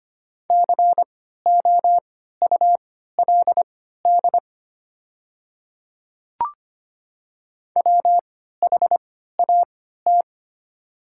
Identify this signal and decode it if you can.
Morse code, heard as a keyed tone, about 25 wpm, spaced as for 14 wpm, 700 Hz. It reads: COULD WHAT
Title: 3x Character Spacing - Top 100 Words - 25wpm